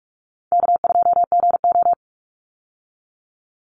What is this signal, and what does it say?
Morse code, heard as a keyed tone, about 44 wpm, 700 Hz. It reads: K2GO
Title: Easy Call Signs - 45wpm